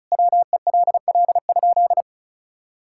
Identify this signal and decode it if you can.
Morse code, heard as a keyed tone, about 34 wpm, 700 Hz. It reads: WELL?
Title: Sentences from Top 1000 Words - Review Part 6 - 35wpm